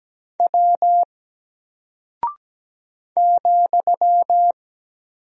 Morse code, heard as a keyed tone, about 17 wpm, 700 Hz. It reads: W ,
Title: Single Character (Letter:Number:Punctuation:Prosign) Alt 17wpm